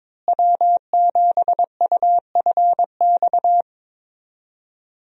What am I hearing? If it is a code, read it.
Morse code, heard as a keyed tone, about 21 wpm, 700 Hz. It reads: W7UFX
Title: USA Call Signs - 5 Characters - Repeated 2x - 22wpm